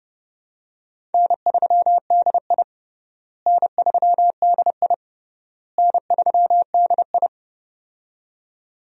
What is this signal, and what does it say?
Morse code, heard as a keyed tone, about 30 wpm, 700 Hz. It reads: N3DI N3DI N3DI
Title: USA Call Signs - 4 Characters - Repeated 3x - 30wpm